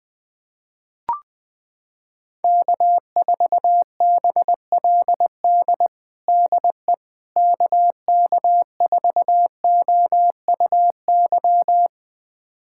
Morse code, heard as a keyed tone, about 20 wpm, 700 Hz. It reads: K4BLD DE KK4OUY
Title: USA Callsign DE Callsign 20wpm